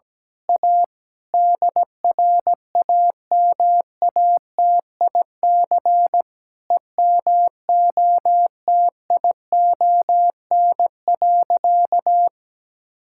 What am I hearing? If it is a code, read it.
Morse code, heard as a keyed tone, about 17 wpm, 700 Hz. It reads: A DRAMATIC EMOTION.
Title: Sentences from Top 1600 Words 17wpm